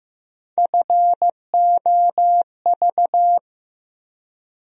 Morse code, heard as a keyed tone, about 15 wpm, 700 Hz. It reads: FOV
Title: USA Call Sign Postfixes - 3 Letters 15wpm